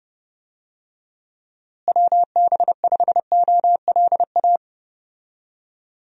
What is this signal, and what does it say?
Morse code, heard as a keyed tone, about 32 wpm, 700 Hz. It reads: WB5OLA
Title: USA Call Signs - 6 Characters - Repeated 2x - 30wpm